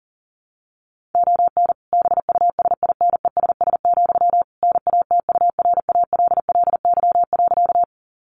Morse code, heard as a keyed tone, about 40 wpm, 700 Hz. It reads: ON BUSINESS, NATURALLY.
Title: Sentences from Top 1300 Words 40wpm